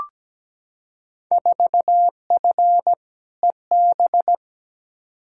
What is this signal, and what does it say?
Morse code, heard as a keyed tone, about 17 wpm, 700 Hz. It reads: 4F EB
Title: Field Day Exchanges - 17wpm